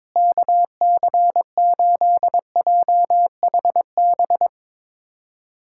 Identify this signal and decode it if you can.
Morse code, heard as a keyed tone, about 22 wpm, 700 Hz. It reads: KC8JHB
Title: USA Callsign DE Callsign 22wpm